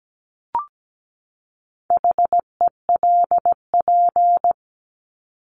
Morse code, heard as a keyed tone, about 17 wpm, 700 Hz. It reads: HELP
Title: Top 200 Words 17wpm